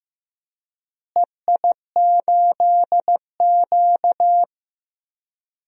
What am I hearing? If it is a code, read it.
Morse code, heard as a keyed tone, about 15 wpm, 700 Hz. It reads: EI8Q
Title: DXCC-100 Call Signs - 4 Characters - Repeated 2x - 15wpm